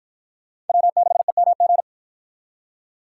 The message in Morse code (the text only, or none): W6EMG